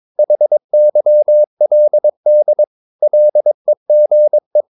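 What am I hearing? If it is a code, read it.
Morse code, heard as a keyed tone, about 22 wpm, 600 Hz. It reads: HYLD LEGE
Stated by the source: Morse practice of short Danish words at 22 wpm, from CW med Gnister.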